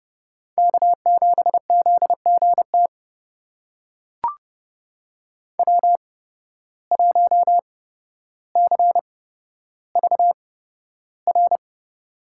Morse code, heard as a keyed tone, about 31 wpm, spaced as for 13 wpm, 700 Hz. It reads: K7ZGT W1CVR
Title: USA Call Signs - 5 Characters - 8x Character Spacing - 30wpm